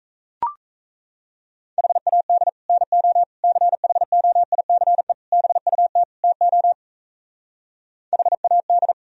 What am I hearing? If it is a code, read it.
Morse code, heard as a keyed tone, about 37 wpm, 700 Hz. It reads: HAD NO CHOICE BUT TO HAD